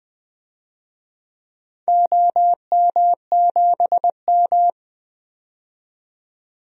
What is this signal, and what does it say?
Morse code, heard as a keyed tone, about 20 wpm, 700 Hz. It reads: OM7M
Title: DXCC-100 Call Signs - 4 Characters - 20wpm